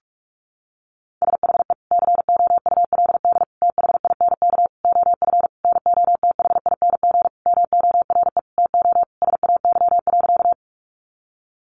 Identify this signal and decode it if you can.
Morse code, heard as a keyed tone, about 44 wpm, 700 Hz. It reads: SHE COULD THINK OF NOTHING MORE TO SAY.
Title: Sentences from Top 500 Words 45wpm